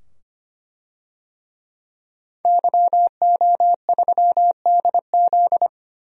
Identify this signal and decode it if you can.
Morse code, heard as a keyed tone, about 25 wpm, 700 Hz. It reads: YO3DZ
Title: DXCC-100 Call Signs - 5 Characters - 25wpm